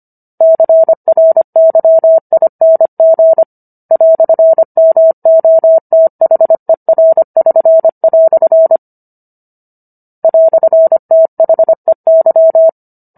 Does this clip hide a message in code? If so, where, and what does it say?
Yes: Morse code from 0.4 to 12.7 s, heard as a keyed tone, about 25 wpm, 650 Hz. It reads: CRYING "MOTHER<SN>" "THEY